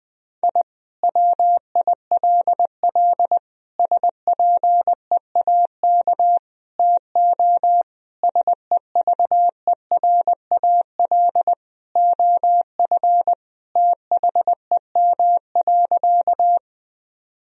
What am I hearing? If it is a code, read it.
Morse code, heard as a keyed tone, about 20 wpm, 700 Hz. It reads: I WILL SPEAK TO SEVERAL OF THEM.
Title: Sentences from Top 1000 Words - Review Part 2 - 20wpm